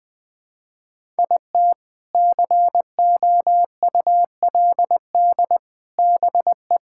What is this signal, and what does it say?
Morse code, heard as a keyed tone, about 20 wpm, 700 Hz. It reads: IT COULD BE
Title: Sentences with 2-Word Phrases - 20wpm